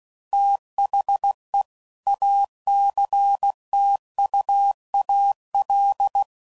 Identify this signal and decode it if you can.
Morse code, heard as a keyed tone, about 16 wpm, 800 Hz. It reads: THE ACTUAL
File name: CW Code Groups @15WPM / Technical Subjects / XC